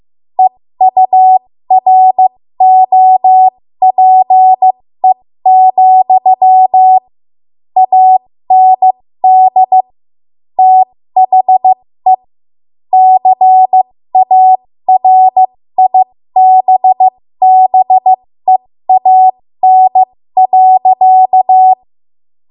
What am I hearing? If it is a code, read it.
Morse code, heard as a keyed tone, about 15 wpm, spaced as for 13 wpm, 750 Hz. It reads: EUROPE, AND THE CARIBBEAN.
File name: Morse / 13WPM